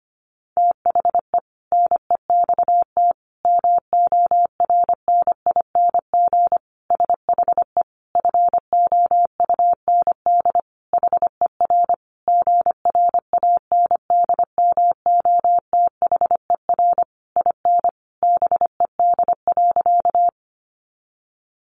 Morse code, heard as a keyed tone, about 25 wpm, 700 Hz. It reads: THE NEXT MORNING SHE FOUND HER GRANDMOTHER IN BED.
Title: Sentences from Top 1200 Words 25wpm